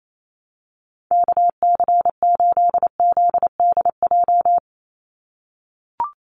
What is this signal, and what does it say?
Morse code, heard as a keyed tone, about 28 wpm, 700 Hz. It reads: KC8ZDJ
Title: USA Call Signs 28wpm